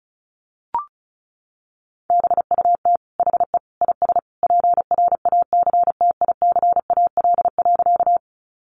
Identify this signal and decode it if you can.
Morse code, heard as a keyed tone, about 34 wpm, 700 Hz. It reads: BUT HE IS PRACTICAL.
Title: Sentences from Top 1400 Words 35wpm